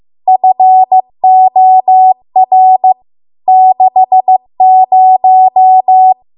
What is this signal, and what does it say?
Morse code, heard as a keyed tone, about 15 wpm, 750 Hz. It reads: FOR 60